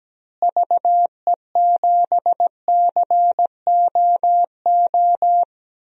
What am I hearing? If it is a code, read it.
Morse code, heard as a keyed tone, about 17 wpm, 700 Hz. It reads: VE7COO
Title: DXCC-100 Call Signs - 6 Characters - Repeated 3x - 17wpm